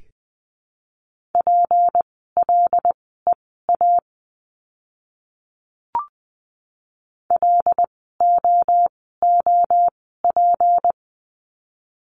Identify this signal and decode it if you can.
Morse code, heard as a keyed tone, about 20 wpm, spaced as for 14 wpm, 700 Hz. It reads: PLEA LOOP
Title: 153 Lesson 25f - Letter p - Words 20wpm